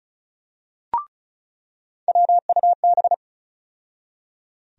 Morse code, heard as a keyed tone, about 36 wpm, 700 Hz. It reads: WUB
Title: USA Call Sign Postfixes - 3 Letters 35wpm